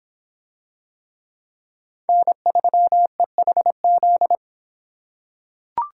Morse code, heard as a keyed tone, about 26 wpm, 700 Hz. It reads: N3EHZ